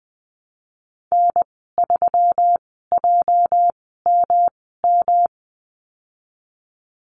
Morse code, heard as a keyed tone, about 20 wpm, spaced as for 15 wpm, 700 Hz. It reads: N3JMM
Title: 207 Lesson 33g - Letter j - Call Signs 20wpm